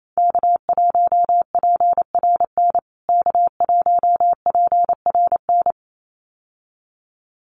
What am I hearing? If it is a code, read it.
Morse code, heard as a keyed tone, about 28 wpm, 700 Hz. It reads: K1PRN K1PRN
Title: USA Call Signs Repeated 2x 28wpm